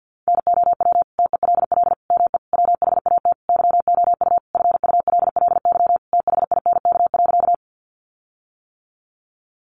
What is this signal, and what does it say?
Morse code, heard as a keyed tone, about 50 wpm, 700 Hz. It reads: NOW TELL ME WHAT YOU FULLY THINK.
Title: Sentences from Top 1300 Words 50wpm